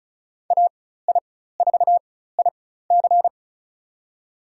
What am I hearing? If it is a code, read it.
Morse code, heard as a keyed tone, about 34 wpm, spaced as for 16 wpm, 700 Hz. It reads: AI4IC
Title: USA Call Signs - 5 Characters - 4x Character Spacing - 35wpm